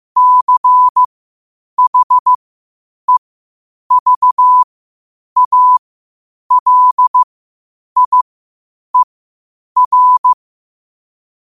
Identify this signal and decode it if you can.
Morse code, heard as a keyed tone, about 15 wpm, spaced as for 8 wpm, 1000 Hz. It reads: CHEVALIER
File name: qotdfiles / 8wpm